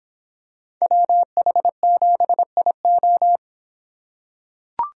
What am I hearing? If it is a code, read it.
Morse code, heard as a keyed tone, about 23 wpm, 700 Hz. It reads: WH7IO